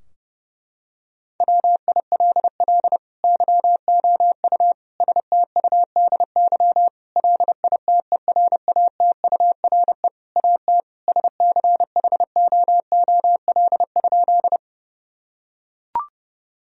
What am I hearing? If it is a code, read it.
Morse code, heard as a keyed tone, about 32 wpm, 700 Hz. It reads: WILL YOU STUDY LITERATURE AT SCHOOL?
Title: Sentences from Top 1000 Words - Review Part 3 - 30wpm